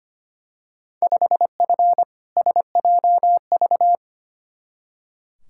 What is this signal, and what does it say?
Morse code, heard as a keyed tone, about 25 wpm, 700 Hz. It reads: 5F SJV
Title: Field Day Exchanges Encore - 25wpm